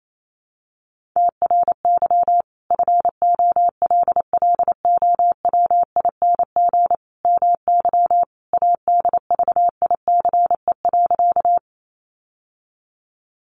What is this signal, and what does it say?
Morse code, heard as a keyed tone, about 28 wpm, 700 Hz. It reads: TRY FOLLOWING MY ADVICE.
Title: Sentences from Top 1100 Words 28wpm